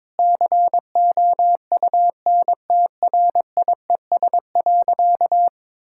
Morse code, heard as a keyed tone, about 22 wpm, 700 Hz. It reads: COUNTRIES.
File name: Difficult Sentences with Hints - 22wpm